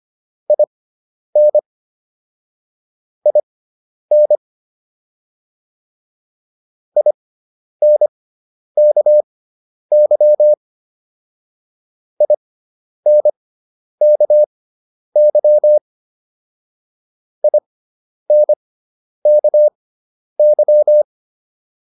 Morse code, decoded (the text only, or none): IN IN INKY INKY INKY